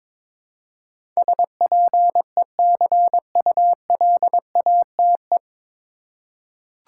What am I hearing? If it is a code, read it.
Morse code, heard as a keyed tone, about 22 wpm, 700 Hz. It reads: SPECULATE
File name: 9 Letter Words - First 3 Letters Given - 22wpm